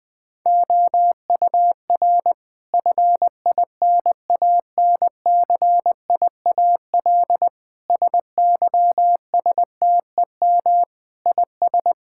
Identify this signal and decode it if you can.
Morse code, heard as a keyed tone, about 20 wpm, 700 Hz. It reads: OUR FINANCIAL SYSTEM IS
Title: Sentences from Top 1000 Words - Review Part 5 - 20wpm